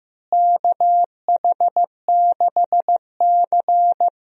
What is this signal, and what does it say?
Morse code, heard as a keyed tone, about 15 wpm, 700 Hz. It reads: KH6C
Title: DXCC-100 Call Signs - 4 Characters - Repeated 3x - 15wpm